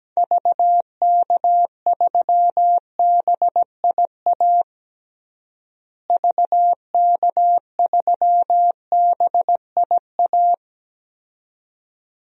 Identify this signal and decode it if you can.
Morse code, heard as a keyed tone, about 17 wpm, 700 Hz. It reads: VK3BIA VK3BIA
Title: DXCC-100 Call Signs - 6 Characters - Repeated 2x - 17wpm